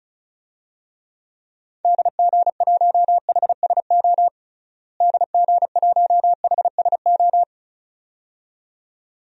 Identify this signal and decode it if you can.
Morse code, heard as a keyed tone, about 34 wpm, 700 Hz. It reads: DG1HSO DG1HSO
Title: DXCC-100 Call Signs - 6 Characters - Repeated 2x - 35wpm